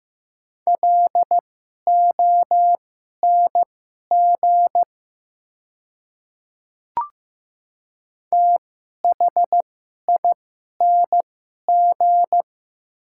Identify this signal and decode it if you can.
Morse code, heard as a keyed tone, about 15 wpm, spaced as for 11 wpm, 700 Hz. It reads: LONG THING
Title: 2x Character Spacing - Top 200 Words - 15wpm